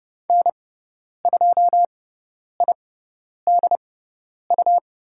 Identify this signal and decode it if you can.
Morse code, heard as a keyed tone, about 30 wpm, spaced as for 10 wpm, 690 Hz. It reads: N2IDU
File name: Lesson 17g - Letter u - Call Signs 30wpm